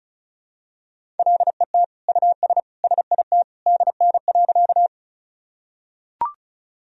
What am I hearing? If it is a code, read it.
Morse code, heard as a keyed tone, about 36 wpm, 700 Hz. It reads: LET US SIT DN.
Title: Sentences from Top 500 Words and Abbreviations 35wpm